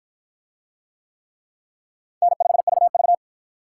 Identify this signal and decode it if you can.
Morse code, heard as a keyed tone, about 53 wpm, 700 Hz. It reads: N5VV